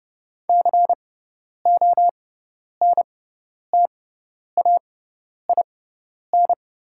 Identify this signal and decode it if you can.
Morse code, heard as a keyed tone, about 30 wpm, spaced as for 10 wpm, 700 Hz. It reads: CONTAIN
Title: 6x Character Spacing - Top 500 Words - 30wpm